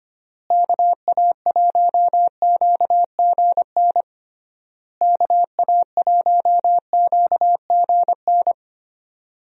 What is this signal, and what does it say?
Morse code, heard as a keyed tone, about 25 wpm, 700 Hz. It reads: KA1QGN KA1QGN
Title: USA Call Signs - 6 Characters - Repeated 3x - 25wpm